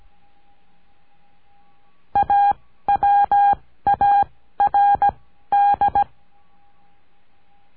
Morse code, heard as a keyed tone, about 17 wpm, spaced as for 13 wpm, 810 Hz. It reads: AWARD
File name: cursos / CURSO DE TELEGRAFIA / Telegrafia - Trilha 14